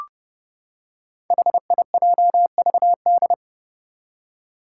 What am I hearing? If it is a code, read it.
Morse code, heard as a keyed tone, about 30 wpm, 700 Hz. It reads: HIJVD
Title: Five Challenging Letter-Number Combinations 30wpm